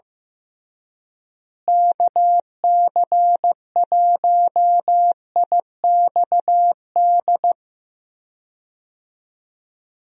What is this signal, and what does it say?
Morse code, heard as a keyed tone, about 15 wpm, 700 Hz. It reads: KC1IXD
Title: USA Call Signs - 6 Characters - Repeated 3x - 15wpm